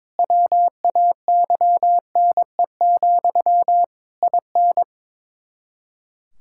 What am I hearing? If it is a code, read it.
Morse code, heard as a keyed tone, about 22 wpm, 700 Hz. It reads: WAYNE, IN
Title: Practice QSO - Exchange 1 - Beginning - 22wpm